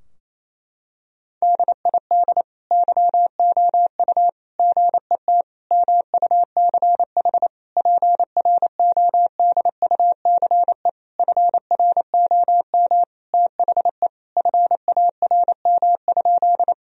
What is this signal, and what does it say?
Morse code, heard as a keyed tone, about 28 wpm, 700 Hz. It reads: DID YOU GET MUCH PRODUCE FROM THE FARM?
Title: Sentences from Top 1000 Words - Review Part 3 - 28wpm